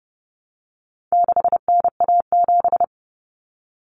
Morse code, heard as a keyed tone, about 30 wpm, 700 Hz. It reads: 6NA7
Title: Common Cut Numbers - 4-Digit 30wpm